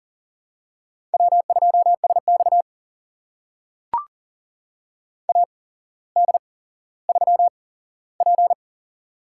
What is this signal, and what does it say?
Morse code, heard as a keyed tone, about 40 wpm, spaced as for 17 wpm, 700 Hz. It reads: W2SX AD3P
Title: USA Call Signs - 4 Characters - 8x Character Spacing - 40wpm